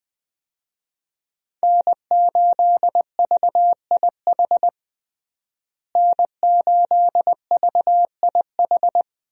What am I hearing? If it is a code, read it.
Morse code, heard as a keyed tone, about 20 wpm, 700 Hz. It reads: N8VIH N8VIH
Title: USA Call Signs - 5 Characters - Repeated 2x - 20wpm